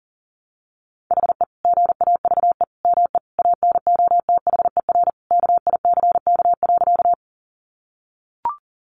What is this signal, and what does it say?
Morse code, heard as a keyed tone, about 40 wpm, 700 Hz. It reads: HE GAVE ME ANOTHER KICK.
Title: Sentences from Top 1300 Words 40wpm